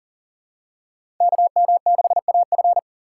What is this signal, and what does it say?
Morse code, heard as a keyed tone, about 40 wpm, 700 Hz. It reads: KM6AF
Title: USA Call Signs - 5 Characters - 3x Character Spacing - 40wpm